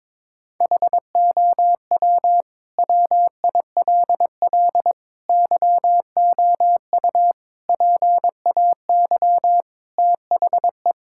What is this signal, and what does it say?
Morse code, heard as a keyed tone, about 22 wpm, 700 Hz. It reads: HOW WILL YOU PAY THE
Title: Sentences from Top 1000 Words - Review Part 5 - 22wpm